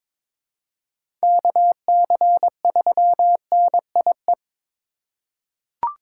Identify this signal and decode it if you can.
Morse code, heard as a keyed tone, about 22 wpm, 700 Hz. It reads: KC3NIE
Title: K1USN SST QSO Practice - 22wpm